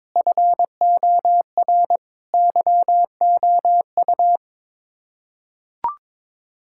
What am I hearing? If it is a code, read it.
Morse code, heard as a keyed tone, about 22 wpm, 700 Hz. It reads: FOR YOU
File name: Sets of 9 Words - 22wpm